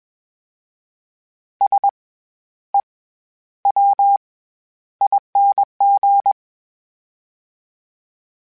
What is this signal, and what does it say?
Morse code, heard as a keyed tone, about 21 wpm, 800 Hz. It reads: S E W ING
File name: Special INGs-mp3s / Special INGs-mp3s / SlowToFaster ing @20